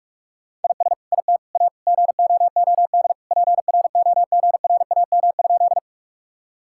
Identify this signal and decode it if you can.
Morse code, heard as a keyed tone, about 42 wpm, 700 Hz. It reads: IS IT A GOOD PROGRAM?
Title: Sentences from Top 1000 Words - Review Part 5 - 45wpm